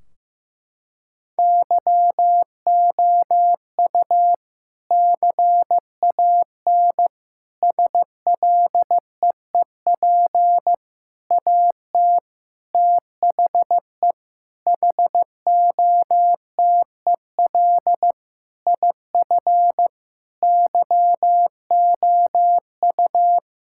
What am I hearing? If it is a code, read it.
Morse code, heard as a keyed tone, about 15 wpm, 700 Hz. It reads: YOU CAN SLEEP AT THE HOTEL IF YOU
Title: Sentences from Top 1600 Words 15wpm